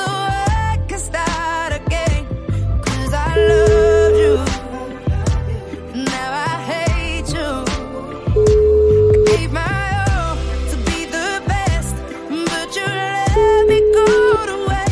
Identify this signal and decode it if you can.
Ringback tone (Europe, 425 Hz), with noise.